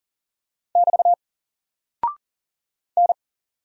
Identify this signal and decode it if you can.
Morse code, heard as a keyed tone, about 39 wpm, 700 Hz. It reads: <BT> N
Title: Single Character (Letter:Number:Punctuation:Prosign) 40wpm